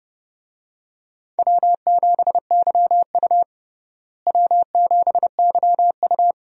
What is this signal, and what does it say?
Morse code, heard as a keyed tone, about 29 wpm, 700 Hz. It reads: W7YU W7YU
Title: USA Call Signs - 4 Characters - Repeated 3x - 30wpm